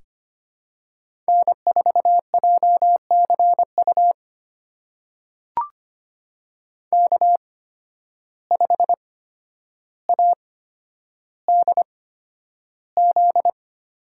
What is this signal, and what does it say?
Morse code, heard as a keyed tone, about 25 wpm, spaced as for 11 wpm, 700 Hz. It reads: N4JCU K5ADZ
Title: USA Call Signs - 5 Characters - 8x Character Spacing - 25wpm